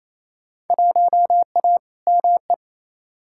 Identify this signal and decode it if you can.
Morse code, heard as a keyed tone, about 28 wpm, 700 Hz. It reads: 1A ME
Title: Field Day Exchanges Encore - 28wpm